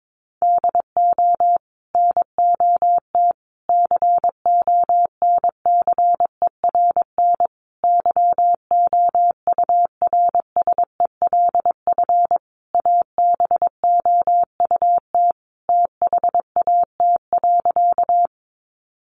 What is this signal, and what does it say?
Morse code, heard as a keyed tone, about 22 wpm, 700 Hz. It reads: DO NOT CONCERN YOURSELF ABOUT THAT.
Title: Sentences from Top 1000 Words 22wpm